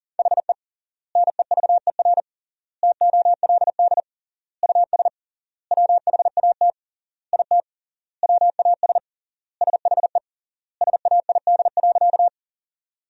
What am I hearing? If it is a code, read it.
Morse code, heard as a keyed tone, about 40 wpm, 700 Hz. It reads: HE NEVER TOLD US WHAT IT WAS SHE SAID.
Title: 3x Word-Spacing - Sentences from Top 400 Words - 40wpm